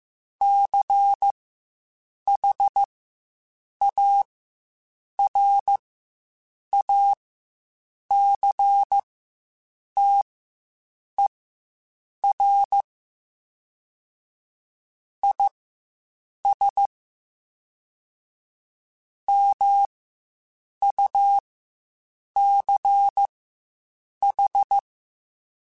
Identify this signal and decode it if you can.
Morse code, heard as a keyed tone, about 15 wpm, spaced as for 7 wpm, 800 Hz. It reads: CHARACTER IS MUCH
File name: Quotes Spaced